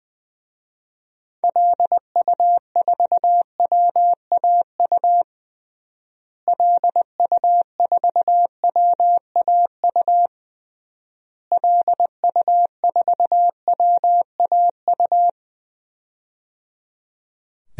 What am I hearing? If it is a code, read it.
Morse code, heard as a keyed tone, about 20 wpm, 700 Hz. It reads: LU4WAU LU4WAU LU4WAU
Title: DXCC-100 Call Signs - 6 Characters - Repeated 3x - 20wpm